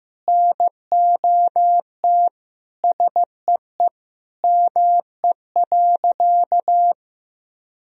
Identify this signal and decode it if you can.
Morse code, heard as a keyed tone, about 15 wpm, 700 Hz. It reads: NOT SEE ME.
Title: Sentences from Top 1000 Words - Review Part 1 - 15wpm